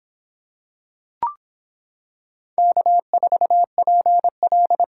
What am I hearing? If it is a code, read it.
Morse code, heard as a keyed tone, about 25 wpm, 700 Hz. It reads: K4PL